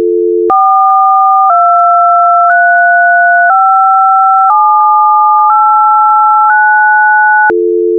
DTMF, heard as telephone touch tones, tones of about 1010 ms. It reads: 4235789